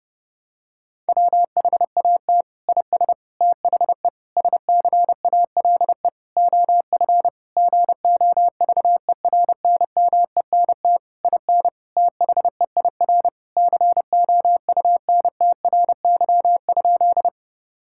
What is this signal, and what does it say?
Morse code, heard as a keyed tone, about 30 wpm, 700 Hz. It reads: WHAT IS THE SCALE OF GOVERNMENT IN THEIR COUNTRY?
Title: Sentences from Top 1000 Words - Review Part 5 - 30wpm